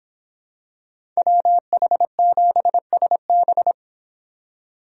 Morse code, heard as a keyed tone, about 26 wpm, 700 Hz. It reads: WH7SB